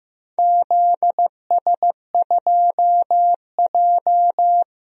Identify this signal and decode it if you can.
Morse code, heard as a keyed tone, about 16 wpm, 700 Hz. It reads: ZS2J
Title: DXCC-100 Call Signs - 4 Characters - Repeated 3x - 15wpm